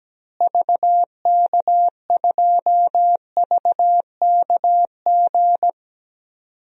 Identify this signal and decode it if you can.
Morse code, heard as a keyed tone, about 17 wpm, 700 Hz. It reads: VK2VKG
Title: DXCC-100 Call Signs - 6 Characters - Repeated 2x - 17wpm